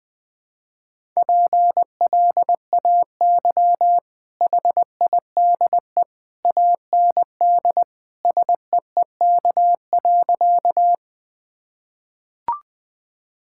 Morse code, heard as a keyed tone, about 20 wpm, 700 Hz. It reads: PLAY HIDE AND SEEK.
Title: Easy Sentences with Hints - 20wpm